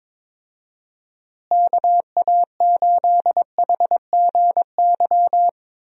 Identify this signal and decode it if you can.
Morse code, heard as a keyed tone, about 22 wpm, 700 Hz. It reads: KA8HGY
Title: USA Call Signs - 6 Characters - Repeated 2x - 22wpm